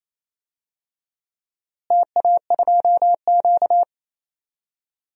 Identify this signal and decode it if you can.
Morse code, heard as a keyed tone, about 28 wpm, 700 Hz. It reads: TA2Q
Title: DXCC-100 Call Signs - 4 Characters - Repeated 2x - 28wpm